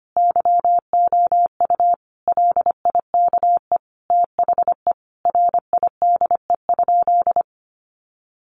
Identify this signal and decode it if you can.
Morse code, heard as a keyed tone, about 25 wpm, 700 Hz. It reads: YOU LIKE THE RIDE?
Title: Sentences from Top 1200 Words 25wpm